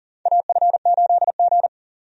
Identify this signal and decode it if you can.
Morse code, heard as a keyed tone, about 40 wpm, 700 Hz. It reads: AF8G
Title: USA Call Signs - 4 Characters - 40wpm